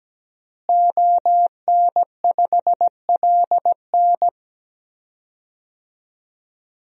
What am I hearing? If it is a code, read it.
Morse code, heard as a keyed tone, about 17 wpm, 700 Hz. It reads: ON5LN
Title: DXCC-100 Call Signs - 5 Characters - Repeated 2x - 17wpm